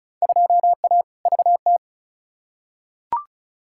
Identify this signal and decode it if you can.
Morse code, heard as a keyed tone, about 36 wpm, 700 Hz. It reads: 2A VT
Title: Field Day Exchanges Encore - 35wpm